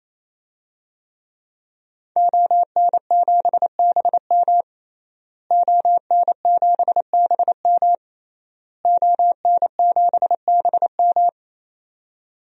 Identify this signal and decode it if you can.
Morse code, heard as a keyed tone, about 28 wpm, 700 Hz. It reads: ON7BM ON7BM ON7BM
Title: DXCC-100 Call Signs - 5 Characters - Repeated 3x - 28wpm